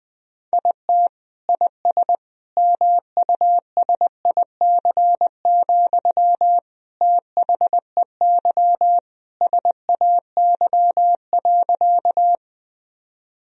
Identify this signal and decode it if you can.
Morse code, heard as a keyed tone, about 20 wpm, 700 Hz. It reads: IT IS MUSIC, THEY SAY.
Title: Sentences from Top 1000 Words - Review Part 1 - 20wpm